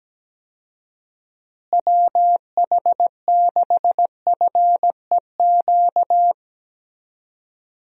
Morse code, heard as a keyed tone, about 17 wpm, 700 Hz. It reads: WH6FEQ
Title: USA Call Signs - 6 Characters - Repeated 2x - 17wpm